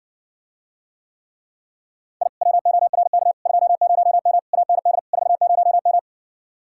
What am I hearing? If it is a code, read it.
Morse code, heard as a keyed tone, about 59 wpm, 700 Hz. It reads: I WORK 20M AND 40M